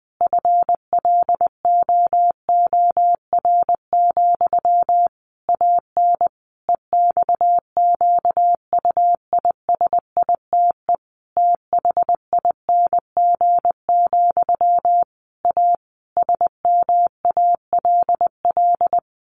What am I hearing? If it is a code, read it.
Morse code, heard as a keyed tone, about 20 wpm, 700 Hz. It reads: FLOOR, AN EXQUISITE THING, A SMALL
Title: A Sound Of Thunder 20wpm